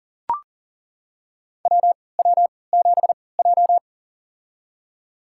Morse code, heard as a keyed tone, about 40 wpm, spaced as for 23 wpm, 700 Hz. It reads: WW7J
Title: USA Call Signs - 4 Characters - 3x Character Spacing - 40wpm